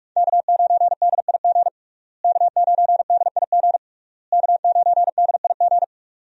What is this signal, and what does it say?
Morse code, heard as a keyed tone, about 47 wpm, 700 Hz. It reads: K9DIG K9DIG K9DIG
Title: USA Call Signs - 5 Characters - Repeated 3x - 45wpm